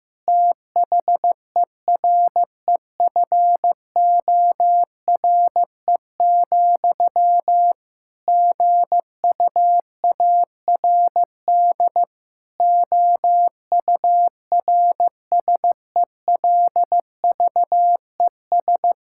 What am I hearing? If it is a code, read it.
Morse code, heard as a keyed tone, about 15 wpm, 700 Hz. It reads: THEREFORE, GUARD OURSELVES
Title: Sentences from Top 1600 Words 15wpm